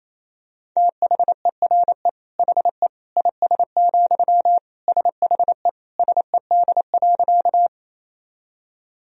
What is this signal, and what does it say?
Morse code, heard as a keyed tone, about 28 wpm, 700 Hz. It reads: THERE HE IS, SHE SED.
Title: Sentences from Top 100 Words and Abbreviations 28wpm